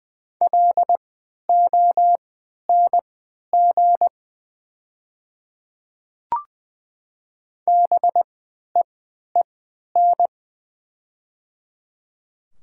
Morse code, heard as a keyed tone, about 20 wpm, spaced as for 11 wpm, 700 Hz. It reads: LONG BEEN
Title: 3x Character Spacing - Top 200 Words - 20wpm